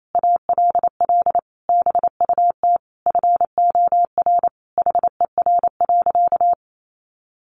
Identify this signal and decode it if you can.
Morse code, heard as a keyed tone, about 28 wpm, 700 Hz. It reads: ALL BUT FOR HER.
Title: Sentences from Top 1400 Words 28wpm